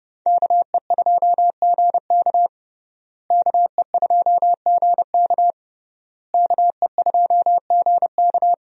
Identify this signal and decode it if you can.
Morse code, heard as a keyed tone, about 30 wpm, 700 Hz. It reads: KE2GK KE2GK KE2GK
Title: USA Call Signs - 5 Characters - Repeated 3x - 30wpm